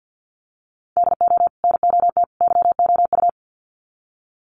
Morse code, heard as a keyed tone, about 50 wpm, 700 Hz. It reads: DO NOT YOU
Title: Sets of 3 Words 50wpm